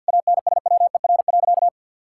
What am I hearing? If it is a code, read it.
Morse code, heard as a keyed tone, about 47 wpm, 700 Hz. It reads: ANSWER.
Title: Sentences from Top 1000 Words - Review Part 6 - 50wpm